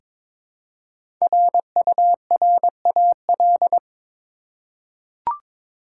Morse code, heard as a keyed tone, about 21 wpm, 690 Hz. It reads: RURAL
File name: 5 Letter Words - First 3 Letters Given - 22wpm